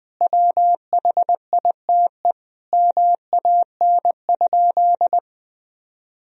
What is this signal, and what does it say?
Morse code, heard as a keyed tone, about 20 wpm, 700 Hz. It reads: WHITE MAN?
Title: Sentences from Top 1000 Words - Review Part 1 - 20wpm